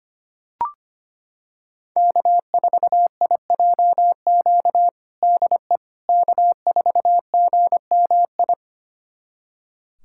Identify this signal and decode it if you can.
Morse code, heard as a keyed tone, about 25 wpm, 700 Hz. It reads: K4IJQ DE K4GMI
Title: USA Callsign DE Callsign 25wpm